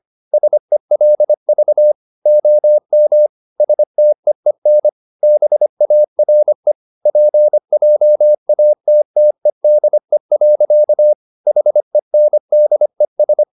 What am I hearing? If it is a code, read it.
Morse code, heard as a keyed tone, about 25 wpm, 600 Hz. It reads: SELV OM STEEN BARE PJATTEDE. HENDES
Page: Ferie 25 wpm